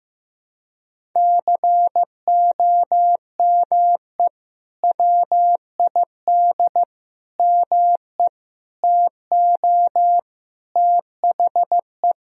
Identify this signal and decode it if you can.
Morse code, heard as a keyed tone, about 15 wpm, 700 Hz. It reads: COME WID ME TO THE
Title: Sentences from Top 200 Words and Abbreviations 15wpm